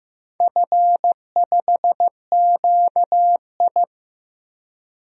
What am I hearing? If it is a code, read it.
Morse code, heard as a keyed tone, about 15 wpm, 700 Hz. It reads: F5QI
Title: DXCC-100 Call Signs - 4 Characters - 15wpm